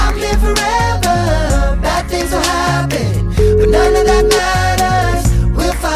Ringback tone (Europe, 425 Hz), with noise.